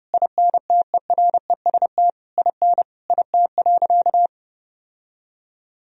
Morse code, heard as a keyed tone, about 32 wpm, 700 Hz. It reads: INTEREST IN IT.
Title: Sentences from Top 1000 Words - Review Part 2 - 30wpm